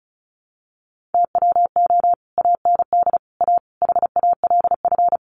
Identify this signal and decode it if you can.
Morse code, heard as a keyed tone, about 34 wpm, 700 Hz. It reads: TWO AND A HALF
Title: Sets of 5 Words 35wpm